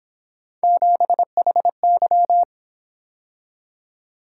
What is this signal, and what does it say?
Morse code, heard as a keyed tone, about 28 wpm, 700 Hz. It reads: 7HY